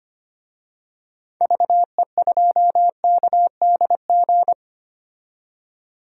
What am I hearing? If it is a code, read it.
Morse code, heard as a keyed tone, about 25 wpm, 700 Hz. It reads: VE2KDG
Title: DXCC-100 Call Signs - 6 Characters - 25wpm